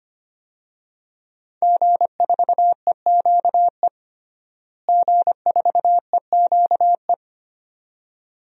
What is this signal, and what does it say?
Morse code, heard as a keyed tone, about 25 wpm, 700 Hz. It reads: G4EQE G4EQE
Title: DXCC-100 Call Signs - 5 Characters - Repeated 2x - 25wpm